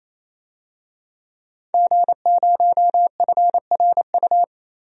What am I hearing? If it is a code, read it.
Morse code, heard as a keyed tone, about 28 wpm, 700 Hz. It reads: G0FRU
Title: DXCC-100 Call Signs - 5 Characters - 28wpm